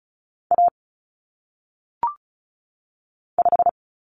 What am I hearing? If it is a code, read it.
Morse code, heard as a keyed tone, about 36 wpm, 700 Hz. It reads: A 5
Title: Common Cut Numbers - 1-Digit 35wpm